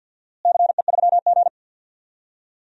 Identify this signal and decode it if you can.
Morse code, heard as a keyed tone, about 51 wpm, 700 Hz. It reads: CE3G
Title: DXCC-100 Call Signs - 4 Characters - Repeated 2x - 50wpm